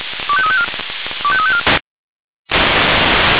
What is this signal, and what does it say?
Five-tone selective call (ZVEI), heard as quick runs of five tones, tones of about 70 ms. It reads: 25354 25354